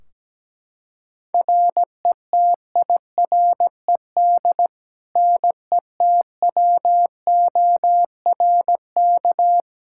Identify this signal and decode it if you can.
Morse code, heard as a keyed tone, about 17 wpm, 700 Hz. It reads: RETIRED NETWORK
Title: Practice QSO - Exchange 3 - More Info - 17wpm